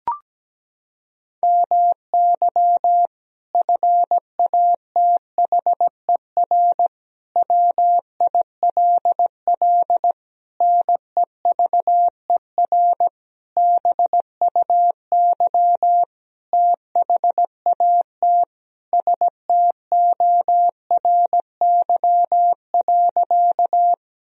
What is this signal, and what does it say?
Morse code, heard as a keyed tone, about 17 wpm, 700 Hz. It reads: MY FATHER WILL NEVER BUY THAT STORY.
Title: Sentences from Top 1000 Words - Review Part 4 - 17wpm